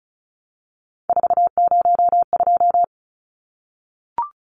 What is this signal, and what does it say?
Morse code, heard as a keyed tone, about 34 wpm, 700 Hz. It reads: 402
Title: Numbers - 3-Digit 35wpm